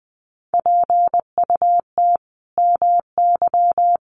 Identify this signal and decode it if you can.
Morse code, heard as a keyed tone, about 19 wpm, 700 Hz. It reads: PUT MY
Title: Sentences from Top 1400 Words 20wpm